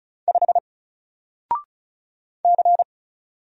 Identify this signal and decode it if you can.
Morse code, heard as a keyed tone, about 36 wpm, 700 Hz. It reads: 5 C
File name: Single-Letter-Number-35wpm